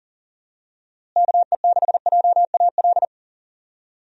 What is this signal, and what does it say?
Morse code, heard as a keyed tone, about 40 wpm, 700 Hz. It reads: KE6JAL
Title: USA Call Signs - 6 Characters - Repeated 2x - 40wpm